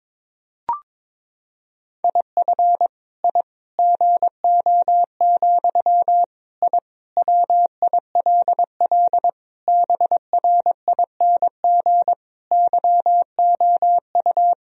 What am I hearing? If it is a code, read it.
Morse code, heard as a keyed tone, about 22 wpm, 700 Hz. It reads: IF I GO, I WILL BRING YOU
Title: Sentences from Top 1000 Words - Review Part 3 - 22wpm